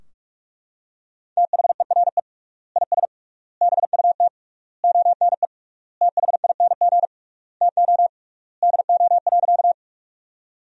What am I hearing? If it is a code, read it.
Morse code, heard as a keyed tone, about 43 wpm, 700 Hz. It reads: THERE IS BUT ONE THING TO DO.
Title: 3x Word-Spacing - Sentences from Top 100 Words - 45wpm